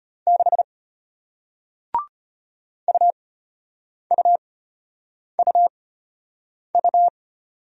Morse code, heard as a keyed tone, about 33 wpm, spaced as for 8 wpm, 700 Hz. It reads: 6 UUUU